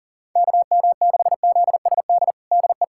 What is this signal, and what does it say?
Morse code, heard as a keyed tone, about 37 wpm, 700 Hz. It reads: KM6ZSD DE
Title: USA Callsign DE Callsign - 6 Characters - 40wpm